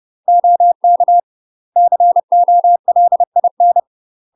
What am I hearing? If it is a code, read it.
Morse code, heard as a keyed tone, about 30 wpm, 700 Hz. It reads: OK COLIN
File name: UK MASTER QSOs TRAINING / 30WPM / UK MASTER QSOs 0044